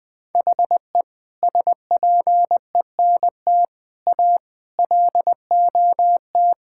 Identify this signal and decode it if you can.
Morse code, heard as a keyed tone, about 20 wpm, 700 Hz. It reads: HE SPENT A LOT
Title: Sentences with 4-Word Phrases - 20wpm